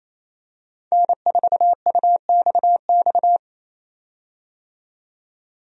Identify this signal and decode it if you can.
Morse code, heard as a keyed tone, about 28 wpm, 700 Hz. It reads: N4UXX
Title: USA Call Signs - 5 Characters - 28wpm